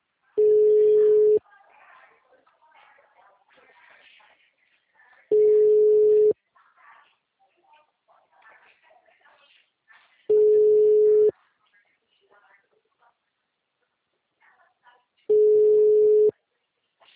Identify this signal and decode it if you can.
Ringback tone (Europe, 425 Hz), with speech.
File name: out- -2002- - - .52310